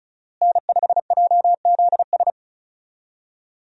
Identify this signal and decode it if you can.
Morse code, heard as a keyed tone, about 36 wpm, 700 Hz. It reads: N5JZS
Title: USA Call Signs - 5 Characters - Repeated 2x - 35wpm